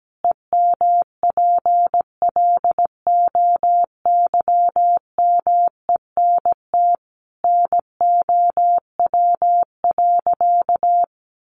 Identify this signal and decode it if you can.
Morse code, heard as a keyed tone, about 17 wpm, 700 Hz. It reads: EMPLOYMENT NOW.
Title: Sentences from Top 1500 Words 17wpm